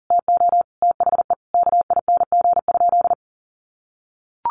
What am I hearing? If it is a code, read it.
Morse code, heard as a keyed tone, about 37 wpm, 700 Hz. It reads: TO THE KING?
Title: Sentences from Top 1400 Words 40wpm